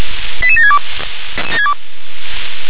Five-tone selective call (ZVEI), heard as quick runs of five tones, tones of about 70 ms. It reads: 88762